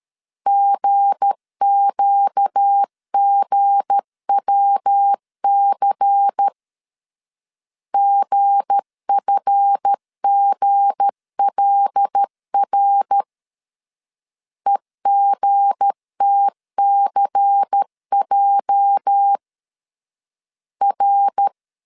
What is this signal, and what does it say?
Morse code, heard as a keyed tone, about 13 wpm, 800 Hz. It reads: GQGWC GFGLR EGTCJ R
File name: CW / DET / 1 9 1500C S B 12wpm 3819